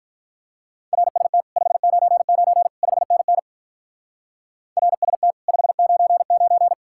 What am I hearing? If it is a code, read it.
Morse code, heard as a keyed tone, about 46 wpm, 700 Hz. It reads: RST 599 5NN RST 599